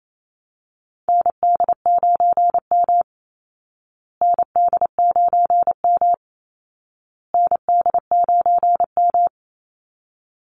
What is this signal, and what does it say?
Morse code, heard as a keyed tone, about 28 wpm, 700 Hz. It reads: ND9M ND9M ND9M
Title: Call Signs Repeated 2x - 28wpm